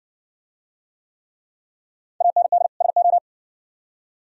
Morse code, heard as a keyed tone, about 57 wpm, 700 Hz. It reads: AND SO